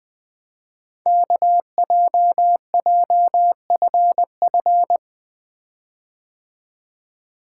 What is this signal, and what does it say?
Morse code, heard as a keyed tone, about 20 wpm, 700 Hz. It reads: KJJFF
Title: Code Groups - 5 Letters - 20wpm